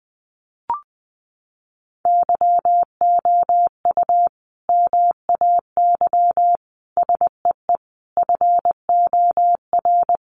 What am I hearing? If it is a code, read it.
Morse code, heard as a keyed tone, about 20 wpm, 700 Hz. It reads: YOU MAY SEE FOR
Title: Sentences from Top 600 Words 20wpm